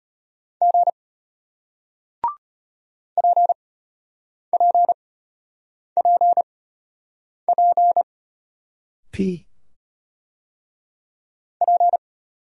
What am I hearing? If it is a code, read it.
Morse code, heard as a keyed tone, about 33 wpm, spaced as for 8 wpm, 700 Hz. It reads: G PPPP P